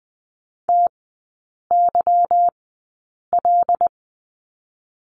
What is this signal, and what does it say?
Morse code, heard as a keyed tone, about 20 wpm, spaced as for 8 wpm, 690 Hz. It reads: TYL
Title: 159 Lesson 26e - Letter g - Three Characters 20wpm